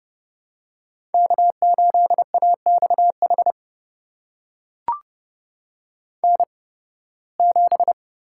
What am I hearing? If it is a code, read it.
Morse code, heard as a keyed tone, about 31 wpm, spaced as for 20 wpm, 700 Hz. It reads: K8AXH N7
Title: USA Call Signs - 5 Characters - 8x Character Spacing - 30wpm